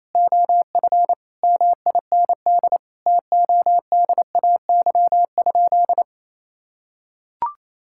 Morse code, heard as a keyed tone, about 28 wpm, 700 Hz. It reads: OF MIND TODAY?
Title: Sentences from Top 1000 Words - Review Part 6 - 28wpm